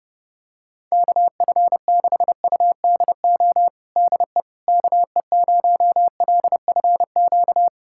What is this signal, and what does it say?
Morse code, heard as a keyed tone, about 32 wpm, 700 Hz. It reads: KF6UDO DE KE0LFQ
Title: USA Callsign DE Callsign 30wpm